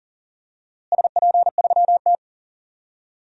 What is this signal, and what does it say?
Morse code, heard as a keyed tone, about 40 wpm, 700 Hz. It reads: SP3T
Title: DXCC-100 Call Signs - 4 Characters - Repeated 3x - 40wpm